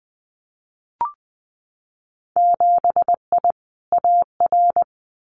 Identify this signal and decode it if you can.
Morse code, heard as a keyed tone, about 20 wpm, 700 Hz. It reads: 7I AR
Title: Winter Field Day Exchanges Encore 20wpm